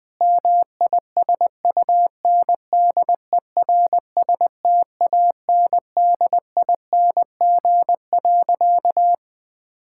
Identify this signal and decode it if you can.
Morse code, heard as a keyed tone, about 21 wpm, 700 Hz. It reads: MISUNDERSTANDING.
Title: Sentences with 4-Word Phrases - Spoken Beginning - 20wpm